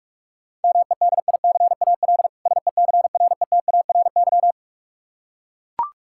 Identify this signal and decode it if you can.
Morse code, heard as a keyed tone, about 44 wpm, 700 Hz. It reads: MEDICAL SECRETARY
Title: Practice QSO - Exchange 3 - More Info - 45wpm